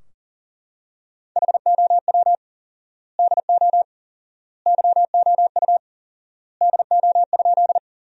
Morse code, heard as a keyed tone, about 40 wpm, 700 Hz. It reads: HOW DO YOU DO?
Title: 4x Word-Spacing - Sentences from Top 100 Words - 40wpm